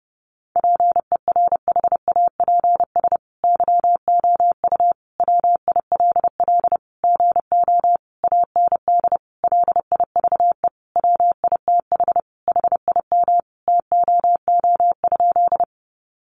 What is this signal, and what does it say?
Morse code, heard as a keyed tone, about 30 wpm, 700 Hz. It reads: PERHAPS YOU WILL GO AND LIVE WITH HIM TOO?
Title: Sentences from Top 500 Words 30wpm